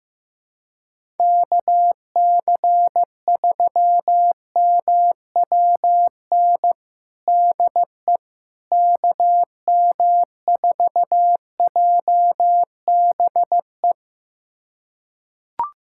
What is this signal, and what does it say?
Morse code, heard as a keyed tone, about 15 wpm, 700 Hz. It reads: KC3MWN DE KM4JBE
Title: USA Callsign DE Callsign - 6 Characters - 15wpm